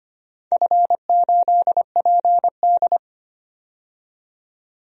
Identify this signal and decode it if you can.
Morse code, heard as a keyed tone, about 25 wpm, 700 Hz. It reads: F8PD
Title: DXCC-100 Call Signs - 4 Characters - 25wpm